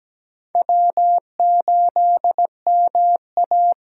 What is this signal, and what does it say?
Morse code, heard as a keyed tone, about 17 wpm, 700 Hz. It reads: W8MA
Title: USA Call Signs - 8x Character Spacing - 17wpm